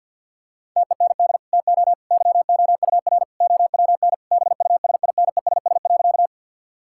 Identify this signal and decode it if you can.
Morse code, heard as a keyed tone, about 50 wpm, 700 Hz. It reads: TEND TO YOUR OWN BUSINESS.
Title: Sentences from Top 1000 Words - Review Part 5 - 50wpm